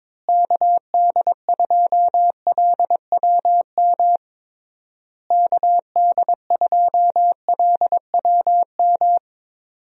Morse code, heard as a keyed tone, about 22 wpm, 700 Hz. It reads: KD2LWM KD2LWM
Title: USA Call Signs - 6 Characters - Repeated 3x - 22wpm